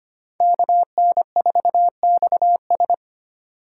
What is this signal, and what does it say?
Morse code, heard as a keyed tone, about 25 wpm, 700 Hz. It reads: KN4XS
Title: USA Call Signs - 5 Characters - 4x Character Spacing - 25wpm